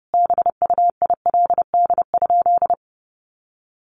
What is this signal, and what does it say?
Morse code, heard as a keyed tone, about 30 wpm, 700 Hz. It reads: BUILD?
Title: Sentences from Top 1300 Words 30wpm